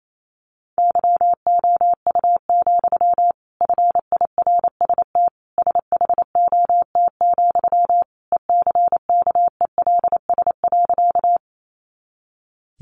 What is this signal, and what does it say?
Morse code, heard as a keyed tone, about 28 wpm, 700 Hz. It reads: YOU, FIRST SHOT, ECKELS.
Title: A Sound Of Thunder 28wpm